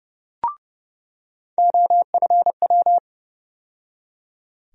Morse code, heard as a keyed tone, about 30 wpm, 700 Hz. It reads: OFW
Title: USA Call Sign Postfixes - 3 Letters 30wpm